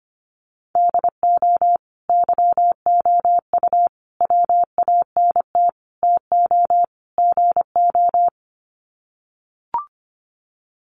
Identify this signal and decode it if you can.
Morse code, heard as a keyed tone, about 25 wpm, 700 Hz. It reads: DO YOU WANT TO GO
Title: Sets of 5 Words Encore2 25wpm